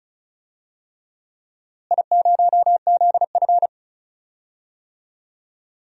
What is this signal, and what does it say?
Morse code, heard as a keyed tone, about 36 wpm, 700 Hz. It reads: I0ZF
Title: DXCC-100 Call Signs - 4 Characters - Repeated 3x - 35wpm